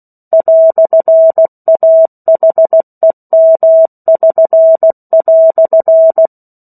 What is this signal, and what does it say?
Morse code, heard as a keyed tone, about 16 wpm, 650 Hz. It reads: "AHEM<SN>"
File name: 03-Chapter-03 16 WPM